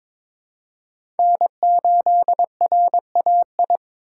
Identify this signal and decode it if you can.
Morse code, heard as a keyed tone, about 22 wpm, 700 Hz. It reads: N8RAI
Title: USA Call Signs - 5 Characters - 8x Character Spacing - 22wpm